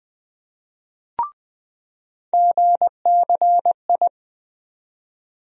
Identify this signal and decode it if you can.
Morse code, heard as a keyed tone, about 20 wpm, 700 Hz. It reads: GCI
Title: USA Call Sign Postfixes - 3 Letters 20wpm